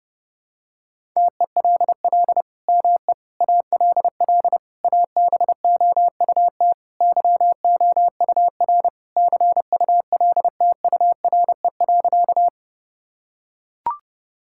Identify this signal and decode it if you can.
Morse code, heard as a keyed tone, about 32 wpm, 700 Hz. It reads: TELL ME ALL ABOUT YOUR CULTURE.
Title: Sentences from Top 1000 Words - Review Part 6 - 30wpm